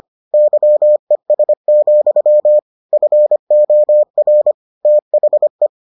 Morse code, heard as a keyed tone, about 25 wpm, 600 Hz. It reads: YES, FOR THE